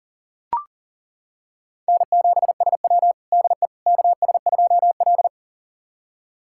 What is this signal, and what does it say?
Morse code, heard as a keyed tone, about 40 wpm, 700 Hz. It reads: N7SW DE KS2L
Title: USA Callsign DE Callsign - 4 Characters - 40wpm